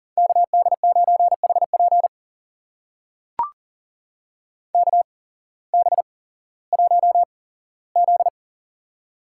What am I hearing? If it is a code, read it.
Morse code, heard as a keyed tone, about 39 wpm, spaced as for 19 wpm, 700 Hz. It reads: KD9HP KB1Z
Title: USA Call Signs - 8x Character Spacing - 40wpm